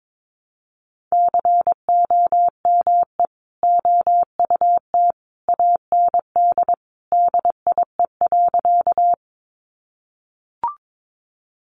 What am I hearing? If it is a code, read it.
Morse code, heard as a keyed tone, about 22 wpm, 700 Hz. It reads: COME OUT AND DIE.
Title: Sentences from Top 1300 Words 22wpm